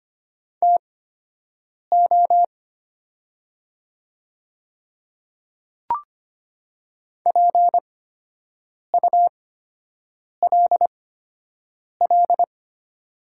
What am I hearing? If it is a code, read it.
Morse code, heard as a keyed tone, about 25 wpm, spaced as for 7 wpm, 700 Hz. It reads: TO PULL
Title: 8x Character Spacing - Top 500 Words - 25wpm